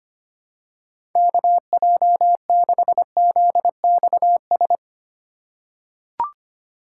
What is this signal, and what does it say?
Morse code, heard as a keyed tone, about 25 wpm, 700 Hz. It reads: KJ6ZXS
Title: USA Call Signs - 2x Character Spacing - 25wpm